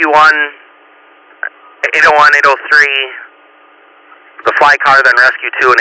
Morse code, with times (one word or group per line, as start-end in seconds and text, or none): none